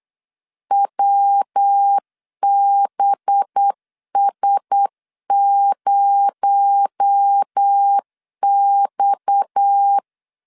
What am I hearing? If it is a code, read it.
Morse code, heard as a keyed tone, about 8 wpm, 800 Hz. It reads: WBS0X
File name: CW / DET / 1 6 1500C S B 8wpm 5142